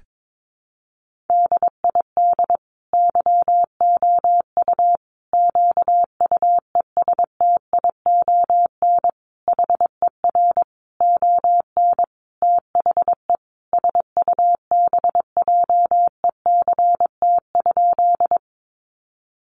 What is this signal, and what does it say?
Morse code, heard as a keyed tone, about 22 wpm, 700 Hz. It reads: DID YOU QUESTION HER ON THE SUBJECT?
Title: Sentences from Top 800 Words 22wpm